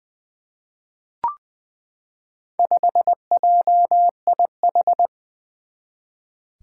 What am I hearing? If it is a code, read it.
Morse code, heard as a keyed tone, about 20 wpm, 700 Hz. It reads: 5JIH
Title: Four Challenging Letter-Number Combinations 20wpm